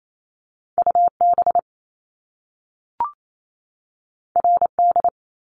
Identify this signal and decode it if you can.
Morse code, heard as a keyed tone, about 28 wpm, 700 Hz. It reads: UB RD
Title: Top Two Letter Combinations in Words 28wpm